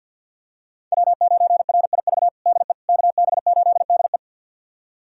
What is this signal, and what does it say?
Morse code, heard as a keyed tone, about 48 wpm, 700 Hz. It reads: W9RIV DE KB8DE
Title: USA Callsign DE Callsign - 5 Characters - 50wpm